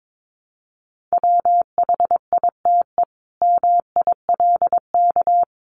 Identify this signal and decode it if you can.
Morse code, heard as a keyed tone, about 22 wpm, 700 Hz. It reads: WHITE MILK
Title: Sentences from Top 1500 Words 22wpm